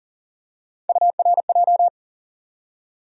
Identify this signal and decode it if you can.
Morse code, heard as a keyed tone, about 40 wpm, 700 Hz. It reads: URJ
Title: USA Call Sign Postfixes - 3 Letters 40wpm